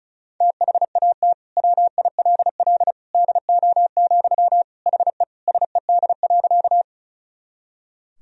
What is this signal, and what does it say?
Morse code, heard as a keyed tone, about 36 wpm, 700 Hz. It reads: THAT WILL DO, HE SED.
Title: Sentences from Top 100 Words and Abbreviations 35wpm